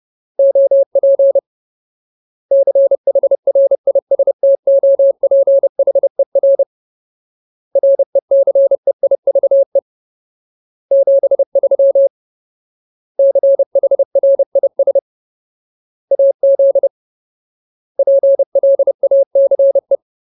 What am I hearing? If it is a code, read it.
Morse code, heard as a keyed tone, about 30 wpm, 550 Hz. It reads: OP CHRISTOPHER RECEIVE 73 CHRIS AZ PLACE
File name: Ultimate Word List / 30WPM Ultimate